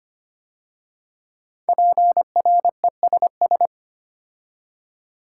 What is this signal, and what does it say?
Morse code, heard as a keyed tone, about 25 wpm, 700 Hz. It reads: PRESS
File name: 5 Letter Words - First 3 Letters Given - 25wpm